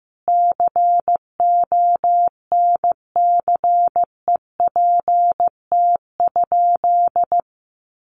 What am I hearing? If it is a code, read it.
Morse code, heard as a keyed tone, about 15 wpm, 700 Hz. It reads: CONCEPT?
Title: Sentences from Top 1100 Words 15wpm